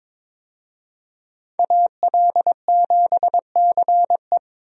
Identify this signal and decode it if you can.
Morse code, heard as a keyed tone, about 22 wpm, 700 Hz. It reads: AL7CE
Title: USA Call Signs - 5 Characters - 22wpm